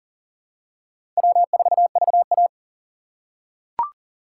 Morse code, heard as a keyed tone, about 40 wpm, 700 Hz. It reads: W4VA
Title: USA Call Signs - 4 Characters - 3x Character Spacing - 40wpm